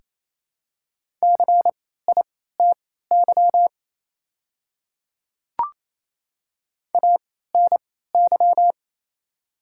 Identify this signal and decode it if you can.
Morse code, heard as a keyed tone, about 28 wpm, spaced as for 16 wpm, 700 Hz. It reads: CITY ANY
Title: 3x Character Spacing - Top 300 Words - 28wpm